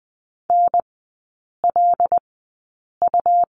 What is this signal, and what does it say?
Morse code, heard as a keyed tone, about 20 wpm, spaced as for 8 wpm, 700 Hz. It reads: NLU
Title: 114 Lesson 19e - Letter w - Three Characters 20wpm